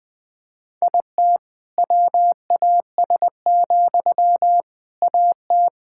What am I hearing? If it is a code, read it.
Morse code, heard as a keyed tone, about 19 wpm, 700 Hz. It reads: IT WAS, AT
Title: Sentences from Top 1000 Words - Review Part 6 - 20wpm